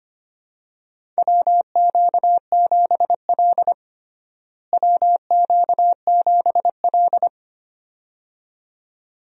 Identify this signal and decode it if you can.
Morse code, heard as a keyed tone, about 25 wpm, 700 Hz. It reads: WQ7L WQ7L
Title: USA Call Signs - 4 Characters - Repeated 2x - 25wpm